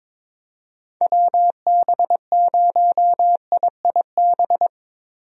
Morse code, heard as a keyed tone, about 22 wpm, 700 Hz. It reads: WB0IIB
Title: USA Call Signs - 6 Characters - Repeated 3x - 22wpm